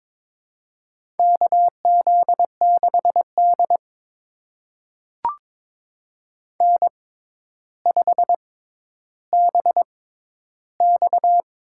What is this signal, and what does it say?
Morse code, heard as a keyed tone, about 22 wpm, spaced as for 11 wpm, 700 Hz. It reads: KZ6D N5BX
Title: USA Call Signs - 4 Characters - 6x Character Spacing - 22wpm